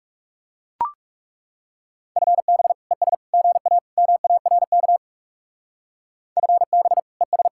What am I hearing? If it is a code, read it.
Morse code, heard as a keyed tone, about 35 wpm, 700 Hz. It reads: FB ES GA MARK FB ES
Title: Speed-Racing Common QSO Phrases Encore - 30wpm